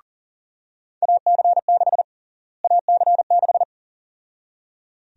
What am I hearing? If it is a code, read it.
Morse code, heard as a keyed tone, about 40 wpm, 700 Hz. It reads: AC6 AC6
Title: USA Call Sign Prefixes - Repeated 2x - 40wpm